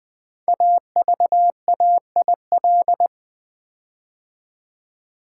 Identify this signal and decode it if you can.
Morse code, heard as a keyed tone, about 20 wpm, 700 Hz. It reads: AVAIL
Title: Build A Word - 5 Letters At A Time - 12-Letter Words - 20wpm